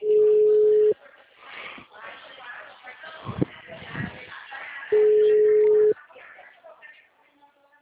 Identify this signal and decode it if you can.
Ringback tone (Europe, 425 Hz), with speech.
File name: out- -2010- - - .58